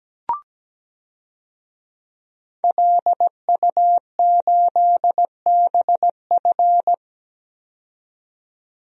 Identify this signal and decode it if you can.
Morse code, heard as a keyed tone, about 17 wpm, 700 Hz. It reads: LU8BF
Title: DXCC-100 Call Signs - 5 Characters - 17wpm